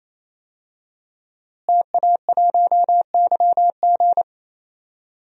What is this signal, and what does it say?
Morse code, heard as a keyed tone, about 28 wpm, 700 Hz. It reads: TA1YG
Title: DXCC-100 Call Signs - 5 Characters - Repeated 3x - 28wpm